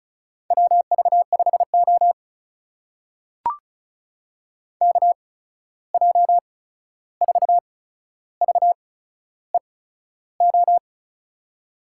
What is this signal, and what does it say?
Morse code, heard as a keyed tone, about 36 wpm, spaced as for 13 wpm, 700 Hz. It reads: WV5O KJ4VEO
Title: USA Call Signs - 8x Character Spacing - 35wpm